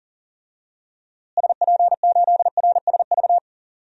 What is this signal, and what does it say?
Morse code, heard as a keyed tone, about 40 wpm, 700 Hz. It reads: SP8RSV
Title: DXCC-100 Call Signs - 6 Characters - Repeated 2x - 40wpm